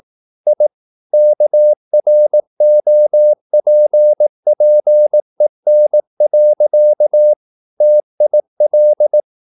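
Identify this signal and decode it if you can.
Morse code, heard as a keyed tone, about 18 wpm, 600 Hz. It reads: I KROPPEN. TIL
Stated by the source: Slow (QRS) Morse practice at 18 wpm, CW med Gnister.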